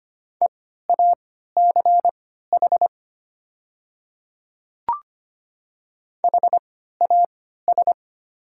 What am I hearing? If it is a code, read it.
Morse code, heard as a keyed tone, about 25 wpm, spaced as for 14 wpm, 700 Hz. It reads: EACH HAS
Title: 3x Character Spacing - Top 100 Words - 25wpm